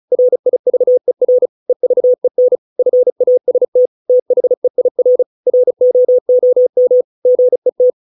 Morse code, heard as a keyed tone, about 35 wpm, 500 Hz. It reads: RIVER EVEN FAST THEIR ROOM GET